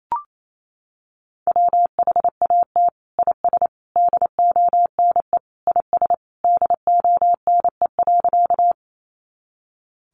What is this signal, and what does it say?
Morse code, heard as a keyed tone, about 28 wpm, 700 Hz. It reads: WHAT IS DONE IS DONE.
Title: Sentences from Top 500 Words 28wpm